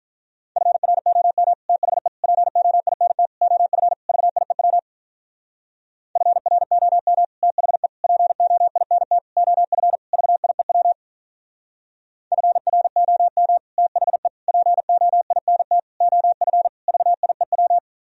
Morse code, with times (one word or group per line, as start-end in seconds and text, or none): 0.6-1.5 FROM
1.7-2.1 THE
2.2-3.3 POINT
3.4-3.9 OF
4.1-4.8 VIEW
6.2-7.2 FROM
7.4-7.9 THE
8.0-9.2 POINT
9.4-10.0 OF
10.1-10.9 VIEW
12.3-13.6 FROM
13.8-14.3 THE
14.5-15.8 POINT
16.0-16.7 OF
16.9-17.8 VIEW